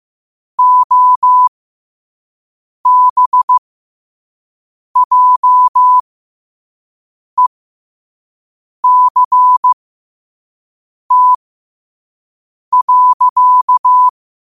Morse code, heard as a keyed tone, about 15 wpm, spaced as for 5 wpm, 1000 Hz. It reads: OBJECT.
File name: qotdfiles / 5wpm